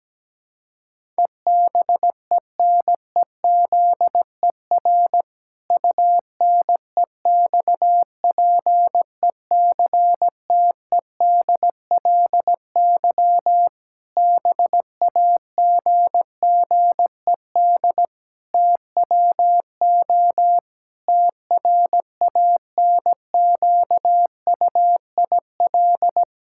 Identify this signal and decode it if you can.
Morse code, heard as a keyed tone, about 17 wpm, 700 Hz. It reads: EBENEZER UNEXPECTEDLY BAGGED TWO TRANQUIL
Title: Pangram Challenge v2 - 17wpm